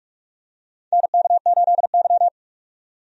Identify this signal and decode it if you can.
Morse code, heard as a keyed tone, about 42 wpm, 700 Hz. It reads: NK8Y
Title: USA Call Signs - 4 Characters - 45wpm